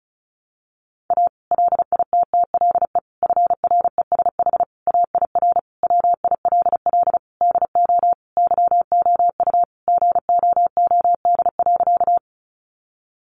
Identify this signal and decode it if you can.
Morse code, heard as a keyed tone, about 34 wpm, 700 Hz. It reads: A LITTLE FRESH AIR WILL DO YOU GOOD.
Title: Sentences from Top 1200 Words 35wpm